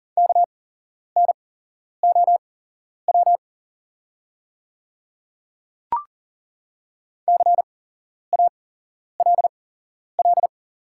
Morse code, heard as a keyed tone, about 40 wpm, spaced as for 11 wpm, 700 Hz. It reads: KNOW CALL
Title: 8x Character Spacing - Top 100 Words - 40wpm